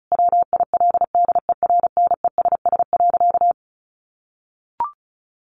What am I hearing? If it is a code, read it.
Morse code, heard as a keyed tone, about 34 wpm, 700 Hz. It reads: WILDERNESS.
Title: A Sound Of Thunder 35wpm